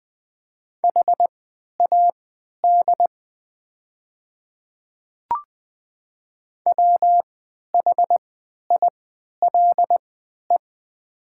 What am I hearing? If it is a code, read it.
Morse code, heard as a keyed tone, about 20 wpm, spaced as for 11 wpm, 700 Hz. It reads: HAD WHILE
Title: 3x Character Spacing - Top 300 Words - 20wpm